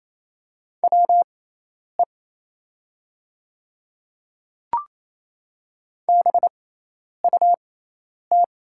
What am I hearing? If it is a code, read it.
Morse code, heard as a keyed tone, about 28 wpm, spaced as for 10 wpm, 700 Hz. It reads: WE BUT
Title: 6x Character Spacing - Top 100 Words - 28wpm